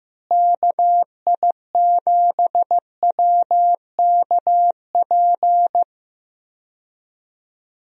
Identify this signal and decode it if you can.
Morse code, heard as a keyed tone, about 15 wpm, 700 Hz. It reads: KI7WKP
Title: USA Call Signs - 6 Characters - Repeated 2x - 15wpm